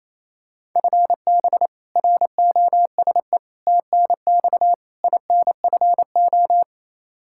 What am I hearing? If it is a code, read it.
Morse code, heard as a keyed tone, about 28 wpm, 700 Hz. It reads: FB ROSE TNX INFO
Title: Practice QSO - Exchange 3 - More Info - 28wpm